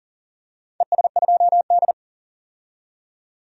Morse code, heard as a keyed tone, about 40 wpm, 700 Hz. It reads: ES2D
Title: DXCC-100 Call Signs - 4 Characters - 40wpm